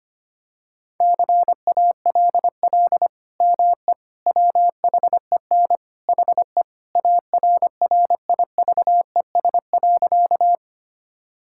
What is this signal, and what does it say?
Morse code, heard as a keyed tone, about 25 wpm, 700 Hz. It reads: CALL ME WHEN HE ARRIVES.
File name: practice / 25wpm / Sentences with 2-Word Phrases - 25wpm